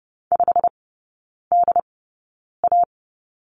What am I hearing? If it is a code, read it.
Morse code, heard as a keyed tone, about 30 wpm, spaced as for 9 wpm, 700 Hz. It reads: 5DA
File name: course / 30:10wpm / 087 Lesson 15e - Number 5 - Three Characters 30wpm